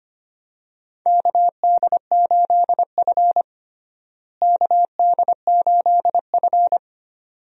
Morse code, heard as a keyed tone, about 25 wpm, 700 Hz. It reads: KD8F KD8F
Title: USA Call Signs - 4 Characters - Repeated 2x - 25wpm